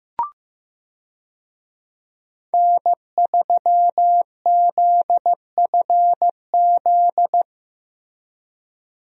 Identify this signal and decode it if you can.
Morse code, heard as a keyed tone, about 15 wpm, 700 Hz. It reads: N3ZFZ
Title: USA Call Signs - 5 Characters - Repeated 2x - 15wpm